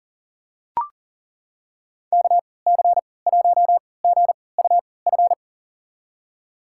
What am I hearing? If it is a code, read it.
Morse code, heard as a keyed tone, about 40 wpm, spaced as for 23 wpm, 700 Hz. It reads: KC1GUF
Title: USA Call Signs - 6 Characters - 3x Character Spacing - 40wpm